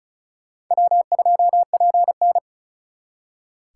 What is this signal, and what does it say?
Morse code, heard as a keyed tone, about 43 wpm, 700 Hz. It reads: W2PN
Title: Speed Racing - USA 1x2 Call Signs 35wpm